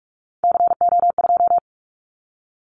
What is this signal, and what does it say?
Morse code, heard as a keyed tone, about 44 wpm, 700 Hz. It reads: CO2
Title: Callsign Prefixes for Top 100 DXCC - 45wpm